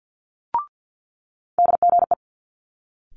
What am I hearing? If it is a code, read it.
Morse code, heard as a keyed tone, about 50 wpm, 700 Hz. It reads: DGE
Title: Multi-letter Phonemes Quick Fire 50wpm